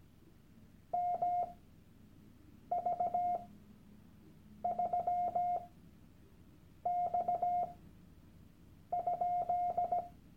Morse code, heard as a keyed tone, about 17 wpm, spaced as for 6 wpm, 710 Hz. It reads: MV3X?
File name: Workshop Dateien / CW-Online-Kurs / Lektion 001